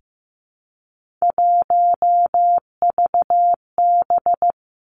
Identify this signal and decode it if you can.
Morse code, heard as a keyed tone, about 15 wpm, 700 Hz. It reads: 1VB
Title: Three Challenging Letter-Number Combinations 15wpm